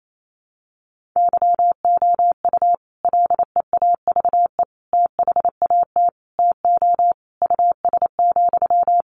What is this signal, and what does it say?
Morse code, heard as a keyed tone, about 28 wpm, 700 Hz. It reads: YOU LEAVE THAT TO US,
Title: Sentences from Top 1400 Words 28wpm